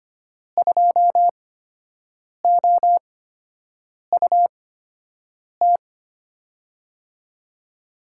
Morse code, heard as a keyed tone, about 25 wpm, spaced as for 7 wpm, 700 Hz. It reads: 2OUT
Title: USA Call Signs - 6 Characters - 8x Character Spacing - 25wpm